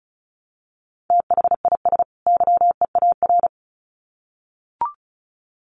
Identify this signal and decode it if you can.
Morse code, heard as a keyed tone, about 34 wpm, 700 Hz. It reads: THIS YEAR
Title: Sets of 2 Words 35wpm